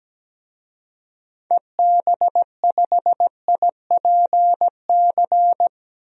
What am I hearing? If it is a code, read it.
Morse code, heard as a keyed tone, about 17 wpm, 700 Hz. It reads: EB5IPC
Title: DXCC-100 Call Signs - 6 Characters - 17wpm